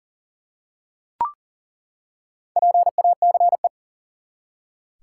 Morse code, heard as a keyed tone, about 42 wpm, 700 Hz. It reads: PACE
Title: Four Letter Words - 40wpm